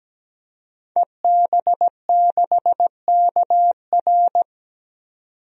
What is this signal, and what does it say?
Morse code, heard as a keyed tone, about 17 wpm, 700 Hz. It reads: EB6KR
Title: DXCC-100 Call Signs - 5 Characters - 17wpm